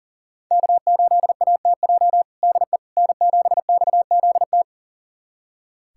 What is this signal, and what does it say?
Morse code, heard as a keyed tone, about 40 wpm, 700 Hz. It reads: K8ATJ DE N7XZT
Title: USA Callsign DE Callsign - 5 Characters - 40wpm